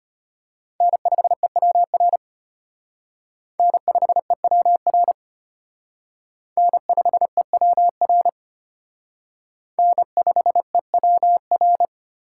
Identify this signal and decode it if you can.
Morse code, heard as a keyed tone, about 32 wpm, 700 Hz. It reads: N5EWR N5EWR N5EWR N5EWR